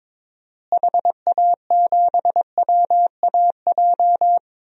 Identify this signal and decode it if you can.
Morse code, heard as a keyed tone, about 22 wpm, 700 Hz. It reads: HA7WAJ
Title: DXCC-100 Call Signs - 6 Characters - Repeated 2x - 22wpm